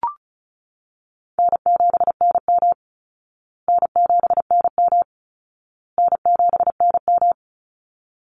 Morse code, heard as a keyed tone, about 34 wpm, 700 Hz. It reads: N7NM N7NM N7NM
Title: Easy Call Signs Repeated 3x - 35wpm